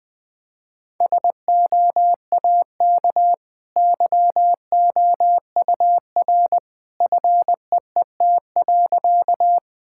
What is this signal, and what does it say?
Morse code, heard as a keyed tone, about 20 wpm, 700 Hz. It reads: SOAK YOUR FEET.
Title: Easy Sentences with Hints - 20wpm